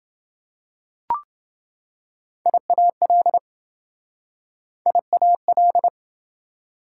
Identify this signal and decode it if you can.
Morse code, heard as a keyed tone, about 26 wpm, 700 Hz. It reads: IAL IAL